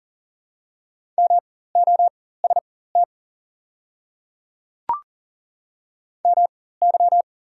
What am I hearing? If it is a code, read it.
Morse code, heard as a keyed tone, about 39 wpm, spaced as for 18 wpm, 700 Hz. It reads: MOST MY
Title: 4x Character Spacing - Top 100 Words - 40wpm